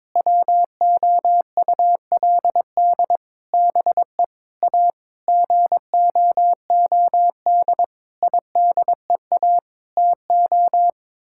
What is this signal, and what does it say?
Morse code, heard as a keyed tone, about 22 wpm, 700 Hz. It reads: WOULD BE A GOOD IDEA TO
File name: practice / 22wpm / Sets of 8 Words - 22wpm